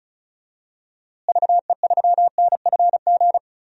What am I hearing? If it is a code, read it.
Morse code, heard as a keyed tone, about 36 wpm, 700 Hz. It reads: VE3NFG
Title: DXCC-100 Call Signs - 6 Characters - Repeated 2x - 35wpm